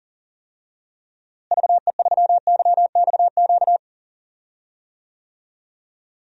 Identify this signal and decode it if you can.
Morse code, heard as a keyed tone, about 40 wpm, 700 Hz. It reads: VE3YXQ
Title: DXCC-100 Call Signs - 6 Characters - Repeated 3x - 40wpm